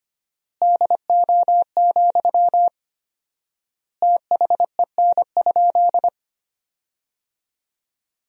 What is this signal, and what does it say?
Morse code, heard as a keyed tone, about 25 wpm, 700 Hz. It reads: DO, THEN?
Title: 4x Word-Spacing - Sentences from Top 100 Words - 25wpm